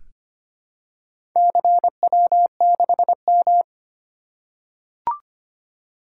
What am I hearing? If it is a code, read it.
Morse code, heard as a keyed tone, about 25 wpm, 700 Hz. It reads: CW6M
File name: practice / 25wpm / Four Letter-Number Combinations - 25wpm